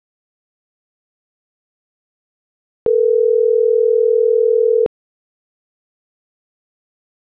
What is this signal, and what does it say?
Ringback tone (North America), 440 + 480 Hz.